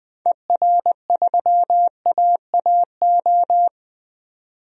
Morse code, heard as a keyed tone, about 20 wpm, 700 Hz. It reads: ER3AAO
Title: DXCC-100 Call Signs - 6 Characters - 20wpm